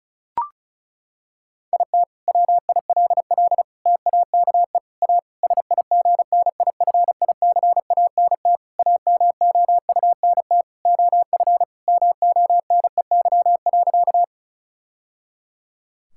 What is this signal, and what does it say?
Morse code, heard as a keyed tone, about 36 wpm, 700 Hz. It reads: IT WILL TAKE A SIGNIFICANT AMOUNT OF MONEY.
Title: Sentences from Top 1000 Words - Review Part 6 - 35wpm